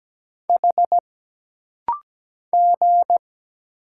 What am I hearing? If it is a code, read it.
Morse code, heard as a keyed tone, about 17 wpm, 700 Hz. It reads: H G
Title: Single Letter-Number - Rapid-Fire 17wpm